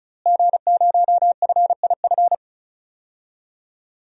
Morse code, heard as a keyed tone, about 36 wpm, 700 Hz. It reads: G0FIF
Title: DXCC-100 Call Signs - 5 Characters - Repeated 3x - 35wpm